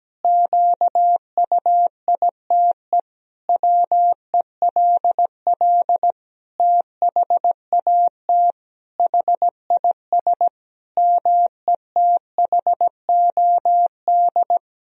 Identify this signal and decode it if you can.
Morse code, heard as a keyed tone, about 17 wpm, 700 Hz. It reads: QUITE WELL THAT HIS METHOD
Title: Sentences from Top 1000 Words - Review Part 3 - 17wpm